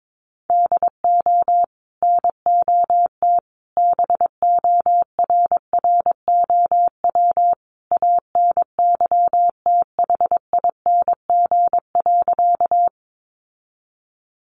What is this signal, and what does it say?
Morse code, heard as a keyed tone, about 22 wpm, 700 Hz. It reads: DO NOT BORROW ANYTHING.
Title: Sentences from Top 1500 Words 22wpm